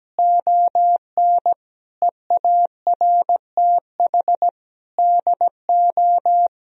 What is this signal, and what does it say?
Morse code, heard as a keyed tone, about 17 wpm, 700 Hz. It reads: ON EARTH DO
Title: Sentences from Top 200 Words and Abbreviations 17wpm